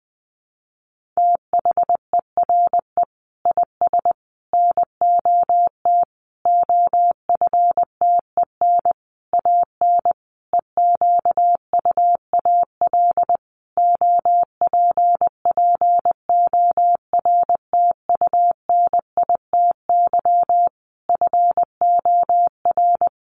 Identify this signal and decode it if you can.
Morse code, heard as a keyed tone, about 20 wpm, 700 Hz. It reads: THERE IS NOT OFTEN AN EQUAL OPPORTUNITY FOR
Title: Sentences from Top 1400 Words 20wpm